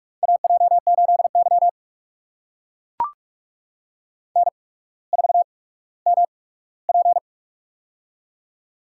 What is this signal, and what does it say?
Morse code, heard as a keyed tone, about 44 wpm, spaced as for 19 wpm, 700 Hz. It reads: AJ8Y N4MP
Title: USA Call Signs - 4 Characters - 8x Character Spacing - 45wpm